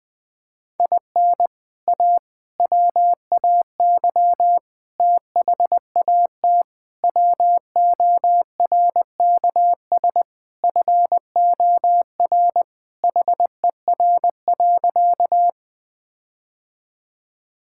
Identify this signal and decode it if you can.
Morse code, heard as a keyed tone, about 20 wpm, 700 Hz. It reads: IN A WAY THAT WORKS FOR HER.
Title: Sentences with 4-Word Phrases - Spoken Beginning - 20wpm